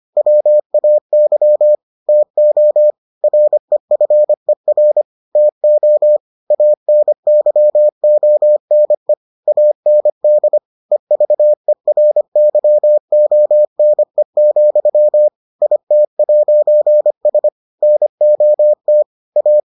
Morse code, heard as a keyed tone, about 25 wpm, 600 Hz. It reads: WAY TO REFER TO ANYONE AND EVERYONE, IT'S NOT A